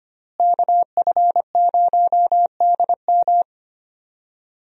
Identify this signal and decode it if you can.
Morse code, heard as a keyed tone, about 25 wpm, 700 Hz. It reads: KF0DM
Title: USA Call Signs - 5 Characters - Repeated 2x - 25wpm